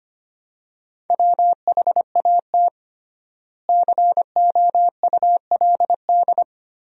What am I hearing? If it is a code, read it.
Morse code, heard as a keyed tone, about 25 wpm, 700 Hz. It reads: WHAT COULD
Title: 3x Word-Spacing - Sentences from Top 100 Words - 25wpm